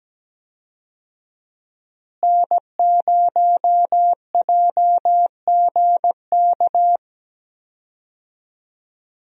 Morse code, heard as a keyed tone, about 17 wpm, 700 Hz. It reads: N0JGK
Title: USA Call Signs - 5 Characters - Repeated 2x - 17wpm